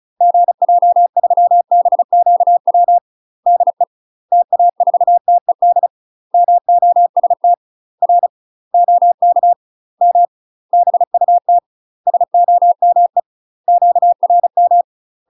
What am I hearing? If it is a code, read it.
Morse code, heard as a keyed tone, about 34 wpm, 700 Hz. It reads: GJ3BQW DE TA4TED MOST R OK M BUT SOME QRM
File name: UK MASTER QSOs TRAINING / 35WPM / UK MASTER QSOs 0015